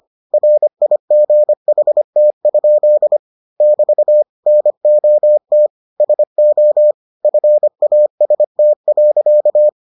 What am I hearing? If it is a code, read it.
Morse code, heard as a keyed tone, about 25 wpm, 600 Hz. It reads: RIGHT? <BT> NOT SO FAST.